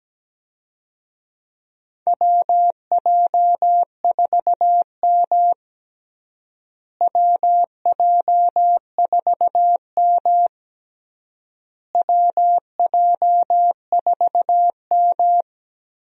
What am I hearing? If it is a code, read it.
Morse code, heard as a keyed tone, about 17 wpm, 700 Hz. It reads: WJ4M WJ4M WJ4M
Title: USA Call Signs - 4 Characters - Repeated 3x - 17wpm